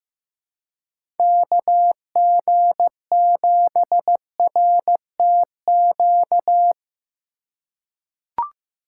Morse code, heard as a keyed tone, about 15 wpm, 700 Hz. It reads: KG7RTQ
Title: USA Call Signs - 6 Characters - 2x Character Spacing - 15wpm